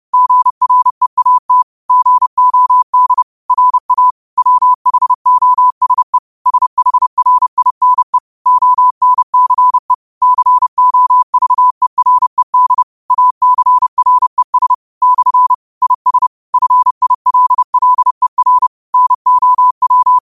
Morse code, heard as a keyed tone, about 30 wpm, 1000 Hz. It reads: GREAT GOD RA WHOSE SHRINE ONCE COVERED ACRES / IS FILLER NOW
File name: qotdfiles / 30wpm